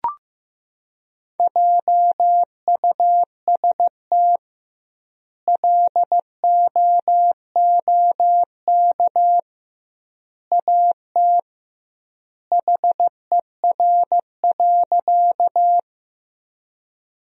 Morse code, heard as a keyed tone, about 15 wpm, 700 Hz. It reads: JUST LOOK AT HER.
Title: 2x Word-Spacing - Sentences from Top 200 Words - 15wpm